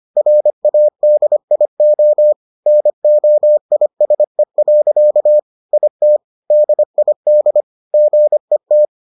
Morse code, heard as a keyed tone, about 25 wpm, 600 Hz. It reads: RADIO NOISE. IT DID GET